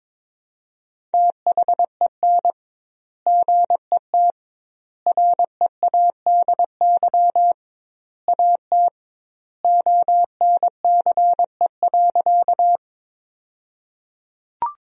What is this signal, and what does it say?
Morse code, heard as a keyed tone, about 22 wpm, 700 Hz. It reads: THEN GET READY AT ONCE.
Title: 2x Word-Spacing - Sentences from Top 400 Words - 22wpm